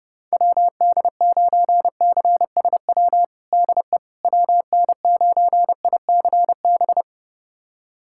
Morse code, heard as a keyed tone, about 30 wpm, 700 Hz. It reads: WD9CSW DE WN9ICB
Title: USA Callsign DE Callsign 30wpm